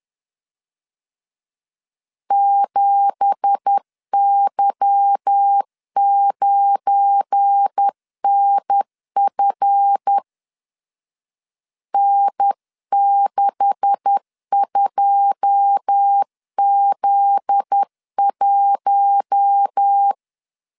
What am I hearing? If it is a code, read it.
Morse code, heard as a keyed tone, about 10 wpm, 800 Hz. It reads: 7Y9NF N62Z1
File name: LEZIONI / 9 2867 10wpm 2733